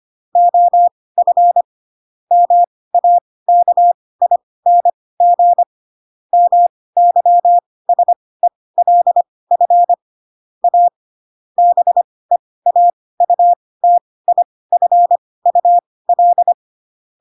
Morse code, heard as a keyed tone, about 25 wpm, spaced as for 18 wpm, 700 Hz. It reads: OF MAKING MYSELF A BEAUTIFUL
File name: Pinocchio / Pinocchio ch2 12 25 ik1hge